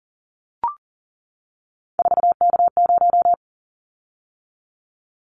Morse code, heard as a keyed tone, about 38 wpm, 690 Hz. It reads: 4K0
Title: Callsign Prefixes for Top 100 DXCC - 40wpm